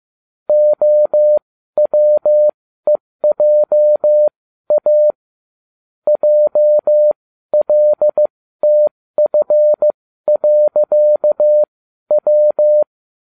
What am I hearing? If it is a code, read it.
Morse code, heard as a keyed tone, about 15 wpm, spaced as for 12 wpm, 600 Hz. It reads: OWEJA JLTF.W